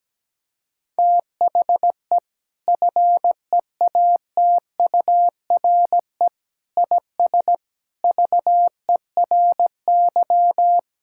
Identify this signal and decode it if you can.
Morse code, heard as a keyed tone, about 17 wpm, 700 Hz. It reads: THE FEATURE IS VERY
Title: Sentences from Top 1000 Words - Review Part 5 - 17wpm